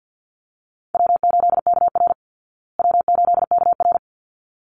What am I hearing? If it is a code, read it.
Morse code, heard as a keyed tone, about 50 wpm, 700 Hz. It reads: W8KR W8KR
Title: Call Signs Repeated 3x - 50wpm